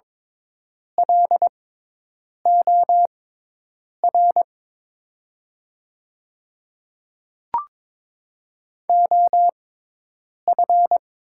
Morse code, heard as a keyed tone, about 22 wpm, spaced as for 8 wpm, 700 Hz. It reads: LOR OF
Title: 6x Character Spacing - Top 300 Words - 22wpm